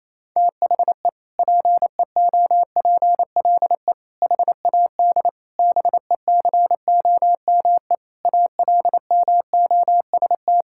Morse code, heard as a keyed tone, about 28 wpm, 700 Hz. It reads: THE PEOPLE HAD BECOME ALMOST
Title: Sentences from Top 1000 Words - Review Part 6 - 28wpm